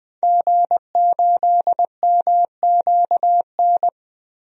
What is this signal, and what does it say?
Morse code, heard as a keyed tone, about 21 wpm, 700 Hz. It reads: G8MQN
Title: DXCC-100 Call Signs - 5 Characters - Repeated 3x - 20wpm